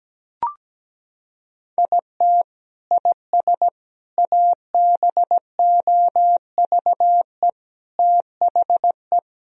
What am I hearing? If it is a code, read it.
Morse code, heard as a keyed tone, about 17 wpm, 700 Hz. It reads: IT IS ABOVE THE
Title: Sentences from Top 1000 Words - Review Part 2 - 17wpm